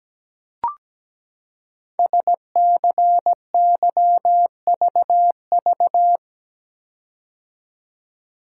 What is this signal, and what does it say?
Morse code, heard as a keyed tone, about 17 wpm, 700 Hz. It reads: SCYVV
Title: Code Groups - 5 Letters - 17wpm